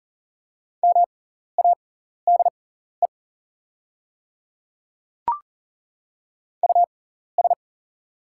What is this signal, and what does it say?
Morse code, heard as a keyed tone, about 40 wpm, spaced as for 14 wpm, 690 Hz. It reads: MADE US
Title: 6x Character Spacing - Top 500 Words - 40wpm